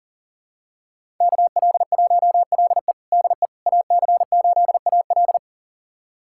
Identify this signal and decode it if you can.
Morse code, heard as a keyed tone, about 40 wpm, 700 Hz. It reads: KL1LE DE AC8AL
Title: USA Callsign DE Callsign - 5 Characters - 40wpm